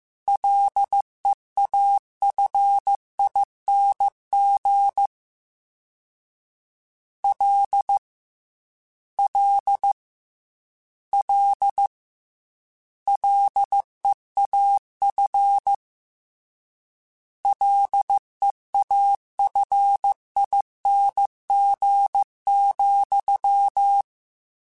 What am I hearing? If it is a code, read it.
Morse code, heard as a keyed tone, about 15 wpm, 800 Hz. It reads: LEAFING L L L LEAF LEAFING,
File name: Combinations / Combinations-mp3s / Combinations@13